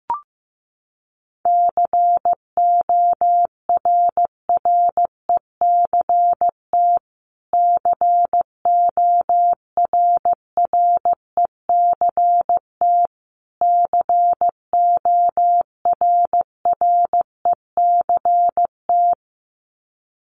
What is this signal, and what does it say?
Morse code, heard as a keyed tone, about 15 wpm, 700 Hz. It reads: CORRECT CORRECT CORRECT
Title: Top 500 Words Repeated 15wpm